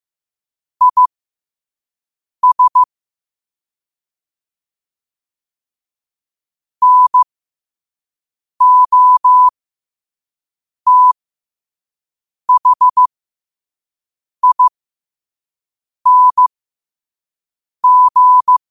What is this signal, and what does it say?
Morse code, heard as a keyed tone, about 15 wpm, spaced as for 5 wpm, 1000 Hz. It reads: IS NOTHING
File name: qotdfiles / 5wpm